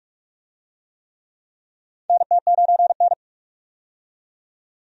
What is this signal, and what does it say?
Morse code, heard as a keyed tone, about 46 wpm, 700 Hz. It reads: NT9N
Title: USA Call Signs - 4 Characters - 45wpm